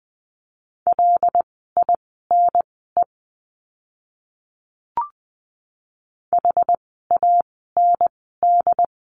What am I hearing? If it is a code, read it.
Morse code, heard as a keyed tone, about 20 wpm, spaced as for 14 wpm, 700 Hz. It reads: LINE HAND
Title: 115 Lesson 19f - Letter w - Words 20wpm